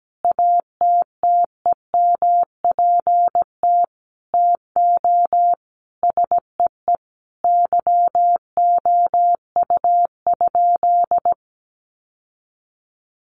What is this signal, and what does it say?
Morse code, heard as a keyed tone, about 17 wpm, 700 Hz. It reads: ATTEMPT TO SEE YOU?
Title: Sentences from Top 1000 Words 17wpm